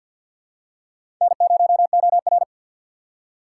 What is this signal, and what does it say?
Morse code, heard as a keyed tone, about 53 wpm, 700 Hz. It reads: N0OR
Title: USA Call Signs - 6x Character Spacing - 50wpm